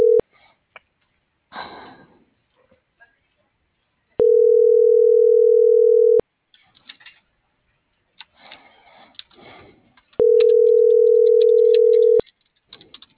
Ringback tone (North America), 440 + 480 Hz.
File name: CENTRAL MULTIMODAL / out cecisandoval